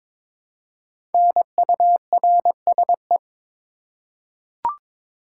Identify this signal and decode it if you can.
Morse code, heard as a keyed tone, about 22 wpm, 700 Hz. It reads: NURSE
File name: 5 Letter Words - First 3 Letters Given - 22wpm